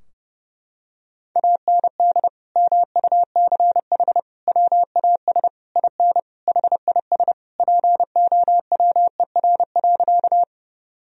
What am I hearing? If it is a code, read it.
Morse code, heard as a keyed tone, about 30 wpm, 700 Hz. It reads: AND MUCH WAS IN HIS POWER.
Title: Sentences from Top 1000 Words - Review Part 2 - 30wpm